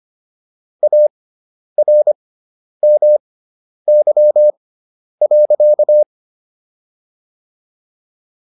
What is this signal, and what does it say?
Morse code, heard as a keyed tone, about 25 wpm, spaced as for 10 wpm, 600 Hz. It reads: ARMY.